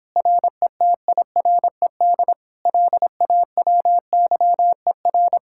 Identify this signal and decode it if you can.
Morse code, heard as a keyed tone, about 31 wpm, 700 Hz. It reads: RETIRED LAWYER